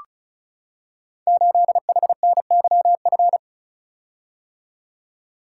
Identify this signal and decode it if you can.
Morse code, heard as a keyed tone, about 36 wpm, 700 Hz. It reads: 8HNYF
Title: Code Groups - 5 Letter:Numbers - 35wpm